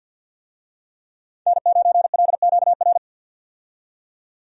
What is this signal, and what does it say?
Morse code, heard as a keyed tone, about 52 wpm, 700 Hz. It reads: N9LQR
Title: USA Call Signs - 5 Characters - 50wpm